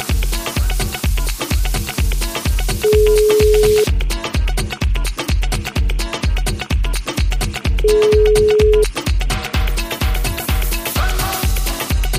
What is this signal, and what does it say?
Ringback tone (Europe, 425 Hz), with noise.